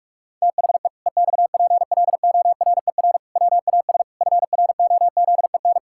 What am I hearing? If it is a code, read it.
Morse code, heard as a keyed tone, about 46 wpm, 700 Hz. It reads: THE EXPLORER WAS FROZEN
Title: Pangram Challenge v2 - 45wpm